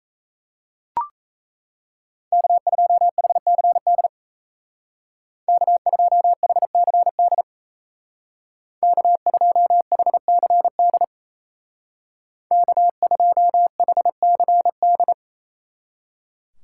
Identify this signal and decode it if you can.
Morse code, heard as a keyed tone, about 36 wpm, 700 Hz. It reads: K2HCD K2HCD K2HCD K2HCD